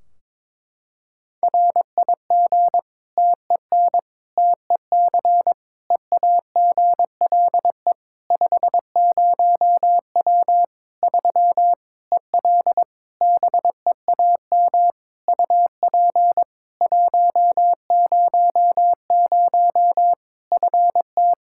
Morse code, heard as a keyed tone, about 22 wpm, 700 Hz. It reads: RIG TEN TEC EAGLE 50W 3 EL BEAM UP 100 FT
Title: Practice QSO - Exchange 2 - Info - 22wpm